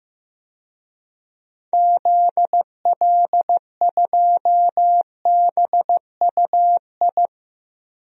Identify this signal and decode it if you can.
Morse code, heard as a keyed tone, about 15 wpm, 700 Hz. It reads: ZL2BUI
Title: DXCC-100 Call Signs - 6 Characters - Repeated 2x - 15wpm